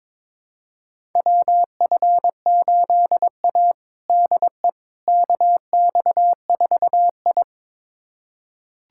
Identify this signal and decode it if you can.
Morse code, heard as a keyed tone, about 22 wpm, 700 Hz. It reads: WF8A DE KX4I
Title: USA Callsign DE Callsign - 4 Characters - 22wpm